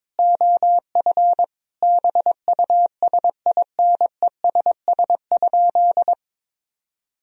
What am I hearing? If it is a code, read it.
Morse code, heard as a keyed tone, about 22 wpm, 700 Hz. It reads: OF BUSINESS?
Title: Sentences from Top 1000 Words - Review Part 4 - 22wpm